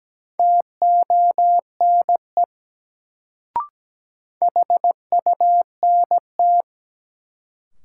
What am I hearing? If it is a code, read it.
Morse code, heard as a keyed tone, about 17 wpm, 700 Hz. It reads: TONE HUNT
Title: Sentences from Top 1600 Words - New Word Drill - 17wpm